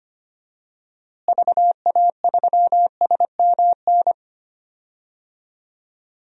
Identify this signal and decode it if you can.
Morse code, heard as a keyed tone, about 25 wpm, 700 Hz. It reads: VA3SMN
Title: DXCC-100 Call Signs - 6 Characters - Repeated 2x - 25wpm